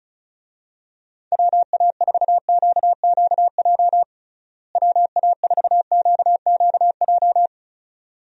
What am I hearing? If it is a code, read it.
Morse code, heard as a keyed tone, about 34 wpm, 700 Hz. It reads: WA4QQJ WA4QQJ
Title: USA Call Signs - 6 Characters - Repeated 2x - 35wpm